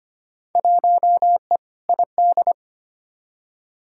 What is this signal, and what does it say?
Morse code, heard as a keyed tone, about 25 wpm, 700 Hz. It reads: 1E ID
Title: Field Day Exchanges - 25wpm